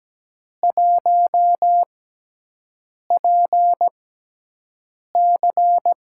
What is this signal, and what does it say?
Morse code, heard as a keyed tone, about 17 wpm, spaced as for 6 wpm, 700 Hz. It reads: 1PC
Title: USA Call Signs - 4 Characters - 6x Character Spacing - 17wpm